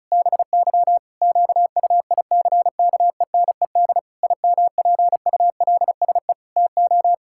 Morse code, heard as a keyed tone, about 36 wpm, 700 Hz. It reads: BY QUICKENED IMPULSE TO
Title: Pangram Challenge v2 - 35wpm